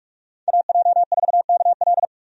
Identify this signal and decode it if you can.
Morse code, heard as a keyed tone, about 46 wpm, 700 Hz. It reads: AJ4KL
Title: USA Call Signs - 5 Characters - 45wpm